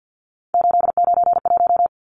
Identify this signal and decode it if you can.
Morse code, heard as a keyed tone, about 50 wpm, 700 Hz. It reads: 891
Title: Numbers - 3-Digit 50wpm